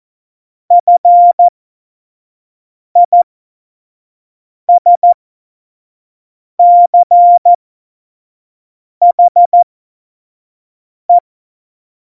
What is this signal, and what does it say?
Morse code, heard as a keyed tone, about 14 wpm, spaced as for 5 wpm, 700 Hz. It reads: FISCHE
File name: Morsekurs / Witze / Witz 30. Oktober 2018, 25 F70